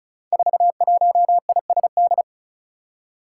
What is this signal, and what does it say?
Morse code, heard as a keyed tone, about 36 wpm, 700 Hz. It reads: 41ISD
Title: Five Challenging Letter-Number Combinations 35wpm